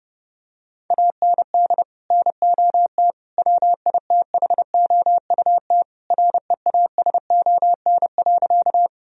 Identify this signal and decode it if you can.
Morse code, heard as a keyed tone, about 32 wpm, 700 Hz. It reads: AND NOT WITHOUT REASON.
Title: Sentences from Top 1000 Words - Review Part 3 - 30wpm